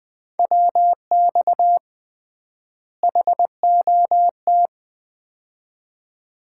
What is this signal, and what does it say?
Morse code, heard as a keyed tone, about 20 wpm, 700 Hz. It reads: WX HOT
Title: Common QSO Phrases - 3x Word-Spacing - 20wpm